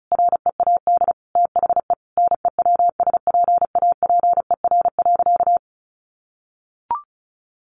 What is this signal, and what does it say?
Morse code, heard as a keyed tone, about 34 wpm, 700 Hz. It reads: READ THE NEWSPAPER.
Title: Sentences from Top 1400 Words 35wpm